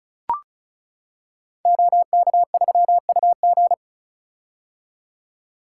Morse code, heard as a keyed tone, about 36 wpm, 700 Hz. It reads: OK3UG
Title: Code Groups - 5 Letter:Numbers - 35wpm